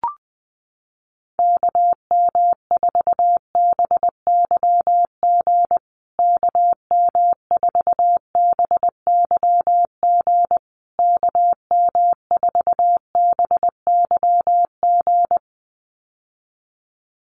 Morse code, heard as a keyed tone, about 20 wpm, 700 Hz. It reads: KM4BYG KM4BYG KM4BYG
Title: USA Call Signs Repeated 3x 20wpm